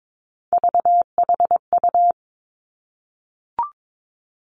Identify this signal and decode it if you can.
Morse code, heard as a keyed tone, about 22 wpm, 700 Hz. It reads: VHU
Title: Three Challenging Letter-Number Combinations 22wpm